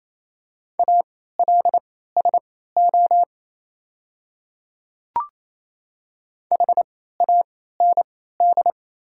Morse code, heard as a keyed tone, about 28 wpm, spaced as for 16 wpm, 700 Hz. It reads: ALSO HAND
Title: 3x Character Spacing - Top 300 Words - 28wpm